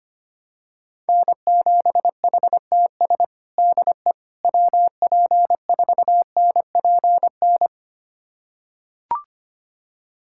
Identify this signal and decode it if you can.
Morse code, heard as a keyed tone, about 25 wpm, 700 Hz. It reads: N7HTS DE WP4NPN
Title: USA Callsign DE Callsign 25wpm